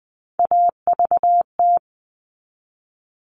Morse code, heard as a keyed tone, about 20 wpm, 700 Hz. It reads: AVT
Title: All Cut Numbers - 3-Digit 20wpm